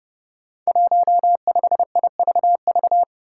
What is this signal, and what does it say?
Morse code, heard as a keyed tone, about 31 wpm, 700 Hz. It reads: 15IVV
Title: Five Challenging Letter-Number Combinations 30wpm